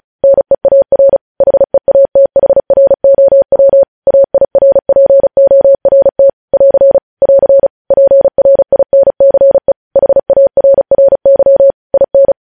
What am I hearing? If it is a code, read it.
Morse code, heard as a keyed tone, about 36 wpm, 550 Hz. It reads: NEAR HEATHROW AIRPORT <AR> <AR> PRINCE HARRY IN
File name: podcast / News headlines-1x-35x35